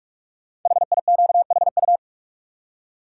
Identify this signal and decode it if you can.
Morse code, heard as a keyed tone, about 44 wpm, 690 Hz. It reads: HIQHU
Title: Code Groups - 5 Letters - 45wpm